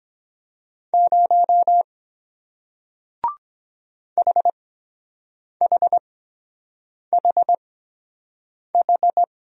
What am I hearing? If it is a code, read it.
Morse code, heard as a keyed tone, about 23 wpm, spaced as for 7 wpm, 700 Hz. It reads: 0 HHHH